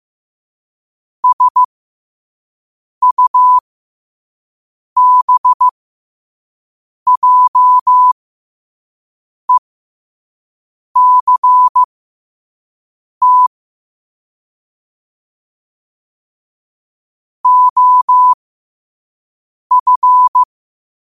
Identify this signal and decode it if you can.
Morse code, heard as a keyed tone, about 15 wpm, spaced as for 5 wpm, 1000 Hz. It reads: SUBJECT OF
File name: qotdfiles / 5wpm